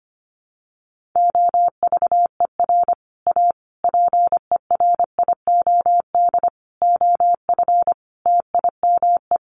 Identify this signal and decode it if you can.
Morse code, heard as a keyed tone, about 25 wpm, 700 Hz. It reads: OVER A PERIOD OF TIME
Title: Sets of 5 Words Encore2 25wpm